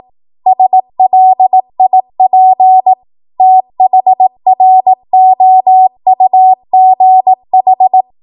Morse code, heard as a keyed tone, about 18 wpm, 750 Hz. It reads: SLIP THROUGH